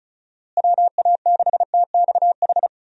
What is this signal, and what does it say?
Morse code, heard as a keyed tone, about 36 wpm, 700 Hz. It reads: WA6TXH
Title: USA Call Signs - 6 Characters - Repeated 2x - 35wpm